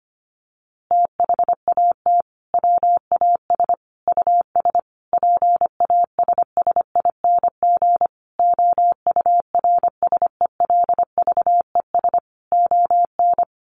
Morse code, heard as a keyed tone, about 25 wpm, 700 Hz. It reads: THAT WAS US PASSING OURSELVES ON
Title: A Sound Of Thunder 25wpm